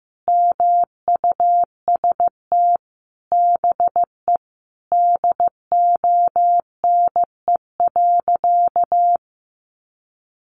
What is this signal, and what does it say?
Morse code, heard as a keyed tone, about 15 wpm, 700 Hz. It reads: MUST BE DONE.
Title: Sentences from Top 1200 Words 15wpm